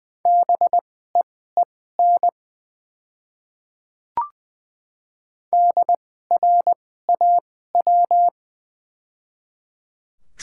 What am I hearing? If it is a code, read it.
Morse code, heard as a keyed tone, about 20 wpm, spaced as for 14 wpm, 700 Hz. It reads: BEEN DRAW
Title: 2x Character Spacing - Top 300 Words - 20wpm